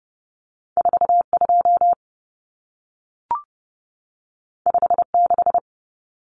Morse code, heard as a keyed tone, about 30 wpm, 700 Hz. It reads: 42 56
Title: Numbers - 2-Digit 30wpm